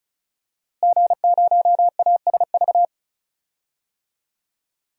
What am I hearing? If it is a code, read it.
Morse code, heard as a keyed tone, about 34 wpm, 700 Hz. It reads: G0ASV
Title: DXCC-100 Call Signs - 5 Characters - Repeated 3x - 35wpm